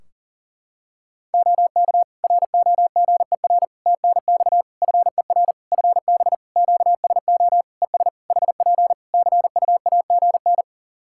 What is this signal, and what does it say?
Morse code, heard as a keyed tone, about 38 wpm, 700 Hz. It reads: OK ROGER TNX FER FB QSO ES HP CUAGN
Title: Practice QSO - Exchange 4 - Ending - 40wpm